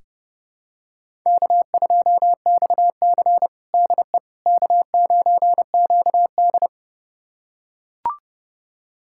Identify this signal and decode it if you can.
Morse code, heard as a keyed tone, about 32 wpm, 700 Hz. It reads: K2XC DE K9QD
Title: USA Callsign DE Callsign - 4 Characters - 30wpm